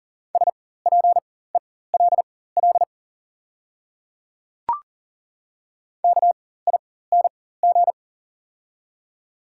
Morse code, heard as a keyed tone, about 40 wpm, spaced as for 19 wpm, 690 Hz. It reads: SPELL KING
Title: 4x Character Spacing - Top 500 Words - 40wpm